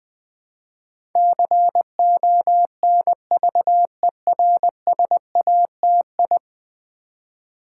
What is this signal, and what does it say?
Morse code, heard as a keyed tone, about 20 wpm, 700 Hz. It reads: CONVERSATI
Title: Build A Word - 5 Letters At A Time - 12-Letter Words - 20wpm